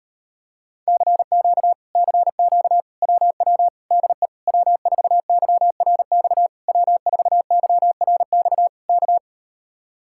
Morse code, heard as a keyed tone, about 38 wpm, 700 Hz. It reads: CQ CQ WW DE W4YRX W4YRX K